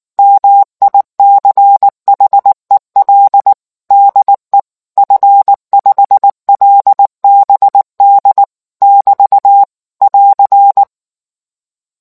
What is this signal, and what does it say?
Morse code, heard as a keyed tone, about 19 wpm, 800 Hz. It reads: MICHEL DE F5LBD <BT> <AR>